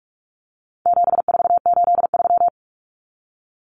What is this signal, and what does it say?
Morse code, heard as a keyed tone, about 44 wpm, 700 Hz. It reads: 7483
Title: Common Cut Numbers - 4-Digit 45wpm